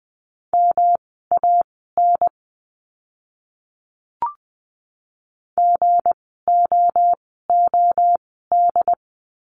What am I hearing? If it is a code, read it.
Morse code, heard as a keyed tone, about 20 wpm, spaced as for 14 wpm, 700 Hz. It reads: MAN GOOD
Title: 237 Lesson 38f - Letter q - Words 20wpm